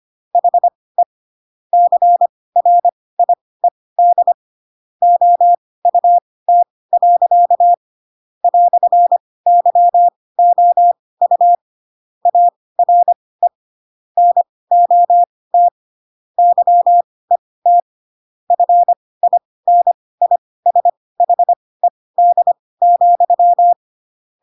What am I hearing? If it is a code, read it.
Morse code, heard as a keyed tone, about 25 wpm, spaced as for 18 wpm, 700 Hz. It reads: HE CRIED OUT. "YOU ARE NOT YET FINISHED,
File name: Pinocchio ch3 12 25 ik1hge